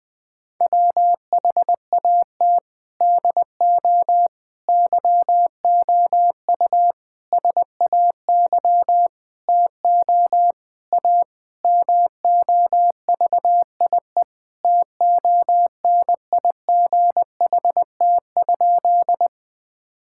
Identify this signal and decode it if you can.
Morse code, heard as a keyed tone, about 19 wpm, 700 Hz. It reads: WHAT DO YOU SAY TO A MOVIE TONIGHT?
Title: Sentences with 5-Word Phrases - 20wpm